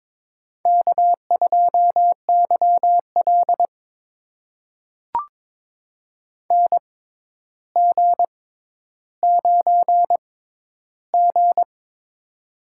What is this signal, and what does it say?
Morse code, heard as a keyed tone, about 23 wpm, spaced as for 11 wpm, 700 Hz. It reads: K2YL NG9G
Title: USA Call Signs - 4 Characters - 6x Character Spacing - 22wpm